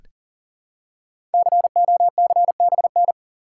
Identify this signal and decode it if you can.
Morse code, heard as a keyed tone, about 40 wpm, 700 Hz. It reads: COCBN
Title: Code Groups - 5 Letters - 40wpm